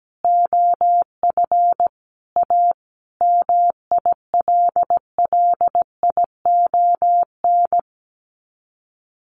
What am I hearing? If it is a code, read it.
Morse code, heard as a keyed tone, about 17 wpm, 700 Hz. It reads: OF A MILLION
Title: Sets of 5 Words Encore2 17wpm